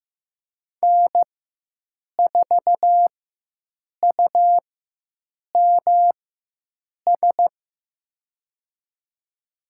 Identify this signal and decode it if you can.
Morse code, heard as a keyed tone, about 15 wpm, spaced as for 7 wpm, 700 Hz. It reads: N4UMS
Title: USA Call Signs - 5 Characters - 4x Character Spacing - 15wpm